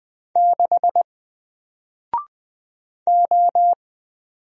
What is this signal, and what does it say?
Morse code, heard as a keyed tone, about 20 wpm, 700 Hz. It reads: 6 O
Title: Single Letter-Number Alt 20wpm